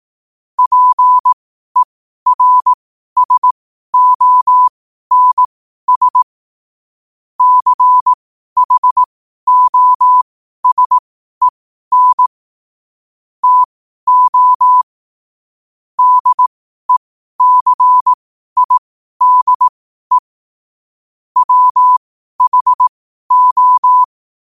Morse code, heard as a keyed tone, about 18 wpm, spaced as for 12 wpm, 1000 Hz. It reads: PERSONS CHOSEN TO DECIDE WHO
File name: qotdfiles / 12wpm